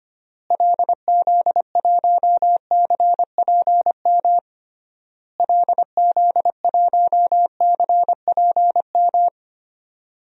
Morse code, heard as a keyed tone, about 25 wpm, 700 Hz. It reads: LZ1CPM LZ1CPM
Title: DXCC-100 Call Signs - 6 Characters - Repeated 3x - 25wpm